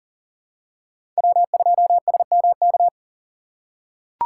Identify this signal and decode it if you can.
Morse code, heard as a keyed tone, about 40 wpm, 700 Hz. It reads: W2SMK
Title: USA Call Signs - 5 Characters - 4x Character Spacing - 40wpm